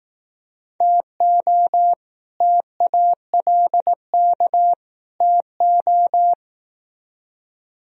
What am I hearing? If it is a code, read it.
Morse code, heard as a keyed tone, about 18 wpm, 700 Hz. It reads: TO TALK TO
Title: Speed-Racing - Sets of 3 Words 15wpm